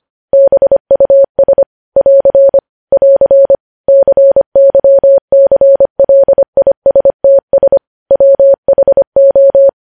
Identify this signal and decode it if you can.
Morse code, heard as a keyed tone, about 25 wpm, 550 Hz. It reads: BUS <AR> <AR> CYCLISTS WHO
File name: podcast / News headlines-1x-25x25